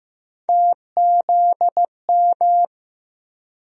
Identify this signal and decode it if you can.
Morse code, heard as a keyed tone, about 15 wpm, 700 Hz. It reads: TZM
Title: USA Call Sign Postfixes - 3 Letters 15wpm